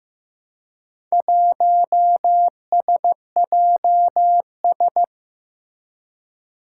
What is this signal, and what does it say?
Morse code, heard as a keyed tone, about 15 wpm, 700 Hz. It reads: 1SJS
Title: Four Challenging Letter-Number Combinations 15wpm